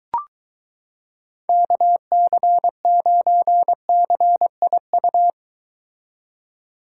Morse code, heard as a keyed tone, about 23 wpm, 700 Hz. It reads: KC9CIU